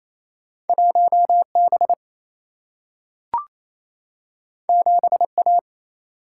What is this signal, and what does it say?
Morse code, heard as a keyed tone, about 28 wpm, 700 Hz. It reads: 1B 7A
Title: Single-Letter Single-Number 28wpm